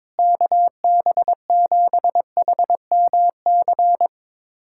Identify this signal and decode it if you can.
Morse code, heard as a keyed tone, about 22 wpm, 700 Hz. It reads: KB7HMC
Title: USA Callsign DE Callsign - 6 Characters - 22wpm